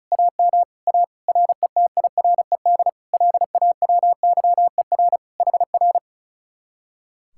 Morse code, heard as a keyed tone, about 44 wpm, 700 Hz. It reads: AM A RETIRED LAWYER HR